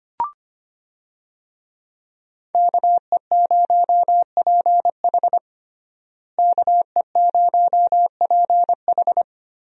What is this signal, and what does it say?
Morse code, heard as a keyed tone, about 25 wpm, 700 Hz. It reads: KE0PH KE0PH
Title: USA Call Signs - 5 Characters - Repeated 3x - 25wpm